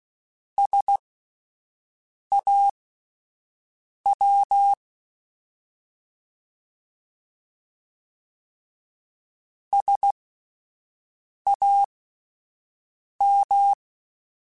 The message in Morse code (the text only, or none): SAW SAM